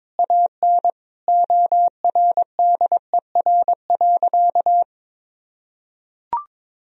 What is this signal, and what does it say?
Morse code, heard as a keyed tone, about 22 wpm, 700 Hz. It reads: AN ORDER.
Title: Sentences from Top 1000 Words - Review Part 2 - 22wpm